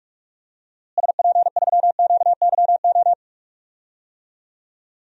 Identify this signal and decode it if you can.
Morse code, heard as a keyed tone, about 43 wpm, 700 Hz. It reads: SP3QYO
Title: DXCC-100 Call Signs - 6 Characters - Repeated 3x - 45wpm